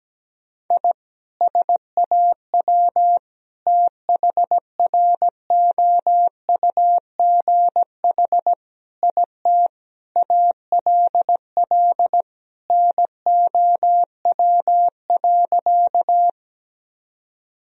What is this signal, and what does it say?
Morse code, heard as a keyed tone, about 17 wpm, 700 Hz. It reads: I SAW THROUGH IT ALL NOW.
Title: Sentences from Top 1000 Words - Review Part 1 - 17wpm